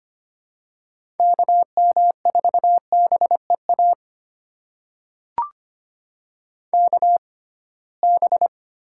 Morse code, heard as a keyed tone, about 25 wpm, spaced as for 19 wpm, 700 Hz. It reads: KM4BEA KB
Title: USA Call Signs - 6 Characters - 6x Character Spacing - 25wpm